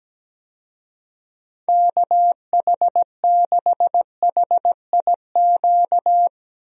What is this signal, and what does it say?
Morse code, heard as a keyed tone, about 17 wpm, 700 Hz. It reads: KH6HIQ
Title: USA Call Signs - 6 Characters - 17wpm